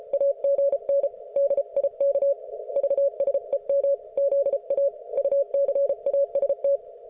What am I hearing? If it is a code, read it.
Morse code, heard as a keyed tone, about 34 wpm, 560 Hz. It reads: AGN DIK VSEM ZA UCAST